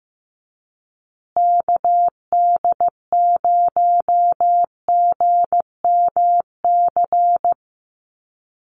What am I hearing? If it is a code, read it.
Morse code, heard as a keyed tone, about 15 wpm, 700 Hz. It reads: KD0GMC
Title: USA Call Signs Repeated 3x 15wpm